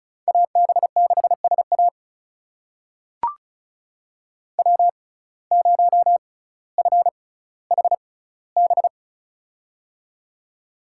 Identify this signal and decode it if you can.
Morse code, heard as a keyed tone, about 36 wpm, spaced as for 18 wpm, 700 Hz. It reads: AB6SA W0FHB
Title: USA Call Signs - 5 Characters - 6x Character Spacing - 35wpm